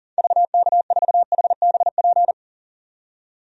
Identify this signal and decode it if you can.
Morse code, heard as a keyed tone, about 40 wpm, 700 Hz. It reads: VK4HBP
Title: DXCC-100 Call Signs - 6 Characters - 40wpm